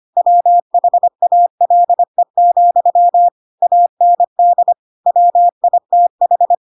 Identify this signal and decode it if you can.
Morse code, heard as a keyed tone, about 25 wpm, 700 Hz. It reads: WHALE, AND WITH